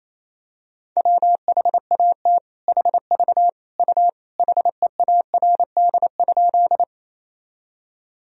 Morse code, heard as a keyed tone, about 28 wpm, 700 Hz. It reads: WHAT HV U HEARD?
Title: Sentences from Top 400 Words and Abbreviations 28wpm